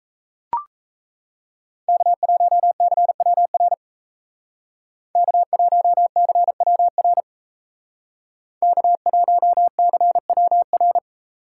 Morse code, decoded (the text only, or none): K1CWR K1CWR K1CWR